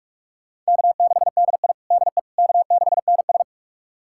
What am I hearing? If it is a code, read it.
Morse code, heard as a keyed tone, about 41 wpm, 700 Hz. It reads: K6DI DE K6NS
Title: USA Callsign DE Callsign - 4 Characters - 45wpm